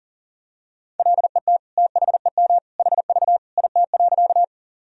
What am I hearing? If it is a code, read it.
Morse code, heard as a keyed tone, about 40 wpm, 700 Hz. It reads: LET THEM HV IT.
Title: Sentences from Top 300 Words and Abbreviations 40wpm